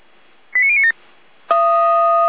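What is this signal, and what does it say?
Five-tone selective call (ZVEI), heard as quick runs of five tones, tones of about 70 ms. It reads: 89097
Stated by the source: ZVEI five-tone selective call 89265, received via SDR.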